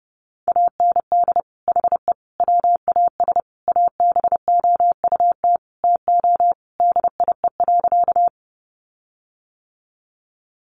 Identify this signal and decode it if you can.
Morse code, heard as a keyed tone, about 30 wpm, 700 Hz. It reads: AND HE WAS ABOUT TO DIE.
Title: Sentences from Top 1300 Words 30wpm